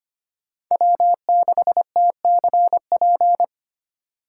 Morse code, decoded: W6TCP

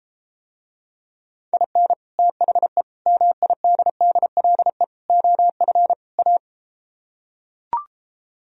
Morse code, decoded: IN THE MIDDLE OF A

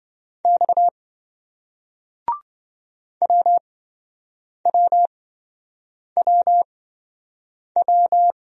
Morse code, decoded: X WWWW

